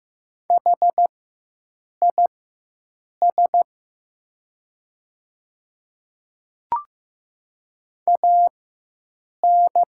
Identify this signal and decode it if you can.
Morse code, heard as a keyed tone, about 15 wpm, spaced as for 7 wpm, 700 Hz. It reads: HIS AN
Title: 4x Character Spacing - Top 100 Words - 15wpm